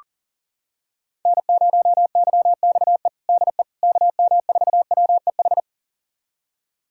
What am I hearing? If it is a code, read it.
Morse code, heard as a keyed tone, about 40 wpm, 700 Hz. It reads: N0YXE DE KM4WEH
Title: USA Callsign DE Callsign 40wpm